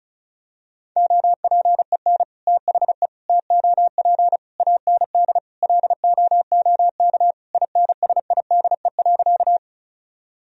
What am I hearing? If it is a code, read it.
Morse code, heard as a keyed tone, about 34 wpm, 700 Hz. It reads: OPEN THE TOP AND LOOK INSIDE.
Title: Sentences from Top 1000 Words - Review Part 2 - 35wpm